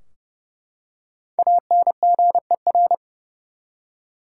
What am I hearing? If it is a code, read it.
Morse code, heard as a keyed tone, about 31 wpm, 700 Hz. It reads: ANGER
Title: Five Letter Words - 30wpm